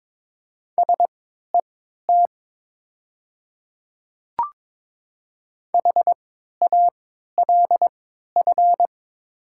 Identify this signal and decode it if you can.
Morse code, heard as a keyed tone, about 22 wpm, spaced as for 12 wpm, 700 Hz. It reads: SET HALF
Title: 3x Character Spacing - Top 500 Words - 22wpm